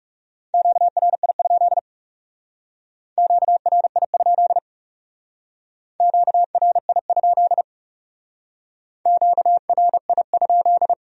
Morse code, heard as a keyed tone, about 37 wpm, 700 Hz. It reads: QRI? QRI? QRI? QRI?
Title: Speed-Racing Q-Codes - 30wpm